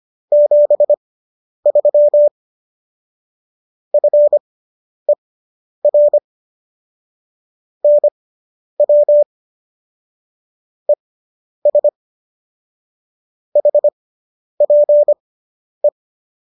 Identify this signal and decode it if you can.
Morse code, heard as a keyed tone, about 25 wpm, spaced as for 10 wpm, 590 Hz. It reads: 73 FER NW ES HPE